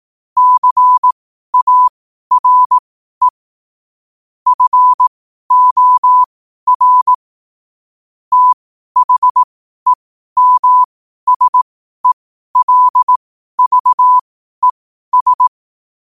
Morse code, heard as a keyed tone, about 18 wpm, spaced as for 12 wpm, 1000 Hz. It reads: CARE FOR THEMSELVES
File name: 12wpm